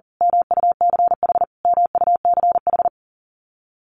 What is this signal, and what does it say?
Morse code, heard as a keyed tone, about 40 wpm, 700 Hz. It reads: MUCH MUCH
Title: Top 500 Words Repeated 40wpm